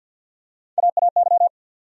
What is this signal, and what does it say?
Morse code, heard as a keyed tone, about 47 wpm, 700 Hz. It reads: AAY
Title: USA Call Sign Postfixes - 3 Letters 50wpm